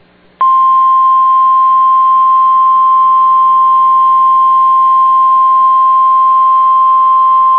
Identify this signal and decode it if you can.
Emergency Alert System (SAME) tones (the 1050 Hz attention tone). It is heard from 1.2 to 6.6 s.